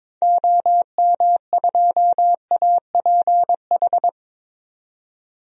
Morse code, heard as a keyed tone, about 22 wpm, 700 Hz. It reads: OM2APH
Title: DXCC-100 Call Signs - 6 Characters - 22wpm